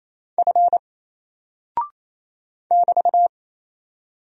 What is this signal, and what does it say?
Morse code, heard as a keyed tone, about 28 wpm, 700 Hz. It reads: F <BT>
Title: Single Character (Letter:Number:Punctuation:Prosign) 28wpm